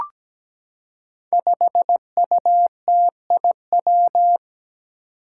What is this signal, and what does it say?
Morse code, heard as a keyed tone, about 17 wpm, 700 Hz. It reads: 5UTIW
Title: Five Challenging Letter-Number Combinations 17wpm